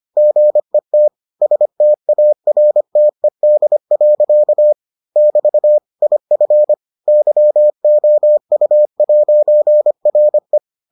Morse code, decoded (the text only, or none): GET STARTED. <BT> IF YOU'RE